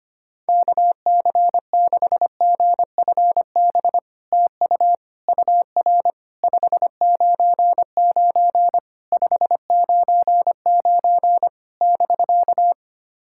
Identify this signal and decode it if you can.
Morse code, heard as a keyed tone, about 25 wpm, 700 Hz. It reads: KC6GFB TU UR 599 599 <BK>
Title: SOTA QSO Practice 25wpm